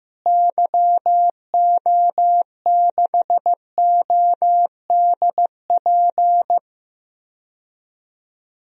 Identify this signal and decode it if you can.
Morse code, heard as a keyed tone, about 15 wpm, 700 Hz. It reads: YO6ODP
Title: DXCC-100 Call Signs - 6 Characters - Repeated 2x - 15wpm